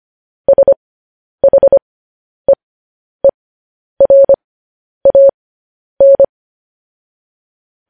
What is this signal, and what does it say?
Morse code, heard as a keyed tone, about 25 wpm, spaced as for 10 wpm, 550 Hz. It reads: SHEERAN
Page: news-headlines-in-morse-code-at-10-wpm